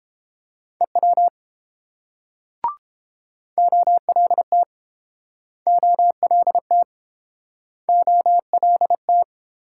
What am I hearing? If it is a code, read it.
Morse code, heard as a keyed tone, about 30 wpm, 700 Hz. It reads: EW OLT OLT OLT